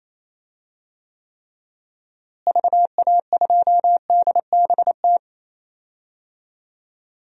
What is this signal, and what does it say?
Morse code, heard as a keyed tone, about 28 wpm, 700 Hz. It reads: VA2DBT
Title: DXCC-100 Call Signs - 6 Characters - Repeated 2x - 28wpm